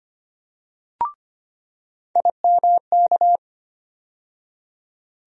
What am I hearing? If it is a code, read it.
Morse code, heard as a keyed tone, about 25 wpm, 700 Hz. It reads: IMK
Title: USA Call Sign Postfixes - 3 Letters 25wpm